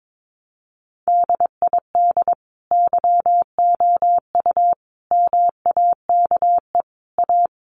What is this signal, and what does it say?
Morse code, heard as a keyed tone, about 22 wpm, 700 Hz. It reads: DID YOU MAKE A
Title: Sentences from Top 1300 Words 22wpm